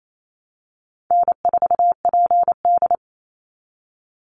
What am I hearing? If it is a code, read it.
Morse code, heard as a keyed tone, about 28 wpm, 700 Hz. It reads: N4PD
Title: Call Signs Repeated 2x - 28wpm